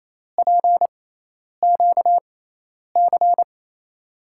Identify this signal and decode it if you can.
Morse code, heard as a keyed tone, about 28 wpm, spaced as for 10 wpm, 700 Hz. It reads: PQC
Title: USA Call Signs - 6 Characters - 6x Character Spacing - 28wpm